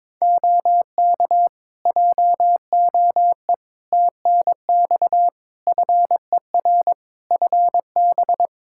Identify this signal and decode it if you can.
Morse code, heard as a keyed tone, about 22 wpm, 700 Hz. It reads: OK JOE TNX FER FB
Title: Practice QSO - Exchange 4 - Ending - 22wpm